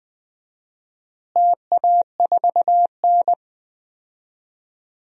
Morse code, heard as a keyed tone, about 20 wpm, 700 Hz. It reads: TA4N
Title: DXCC-100 Call Signs - 4 Characters - Repeated 3x - 20wpm